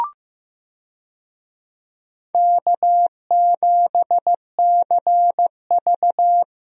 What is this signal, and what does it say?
Morse code, heard as a keyed tone, about 16 wpm, 700 Hz. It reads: K7CV
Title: USA Call Signs - 4 Characters - Repeated 3x - 15wpm